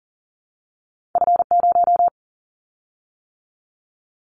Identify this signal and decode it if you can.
Morse code, heard as a keyed tone, about 41 wpm, 700 Hz. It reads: F0
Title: Callsign Prefixes for Top 50 DXCC - 40wpm